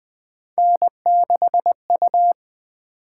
Morse code, heard as a keyed tone, about 20 wpm, 700 Hz. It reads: N6U
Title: Three Letter-Number Combinations - 20wpm